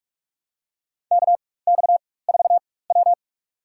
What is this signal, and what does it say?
Morse code, heard as a keyed tone, about 46 wpm, spaced as for 21 wpm, 700 Hz. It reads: KX4W
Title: USA Call Signs - 4 Characters - 4x Character Spacing - 45wpm